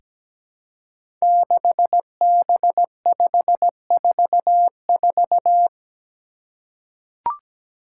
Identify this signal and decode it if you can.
Morse code, heard as a keyed tone, about 17 wpm, 700 Hz. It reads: 6B544
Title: Five Challenging Letter-Number Combinations 17wpm